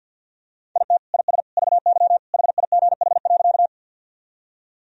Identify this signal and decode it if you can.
Morse code, heard as a keyed tone, about 50 wpm, 700 Hz. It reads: IT IS VY HIGH.
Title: Sentences from Top 200 Words and Abbreviations 50wpm